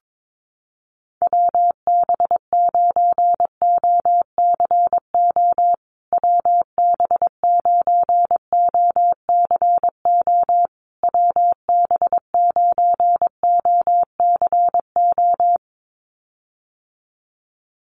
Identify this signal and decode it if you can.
Morse code, heard as a keyed tone, about 22 wpm, 700 Hz. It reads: WB9OCO WB9OCO WB9OCO
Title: USA Call Signs Repeated 3x 22wpm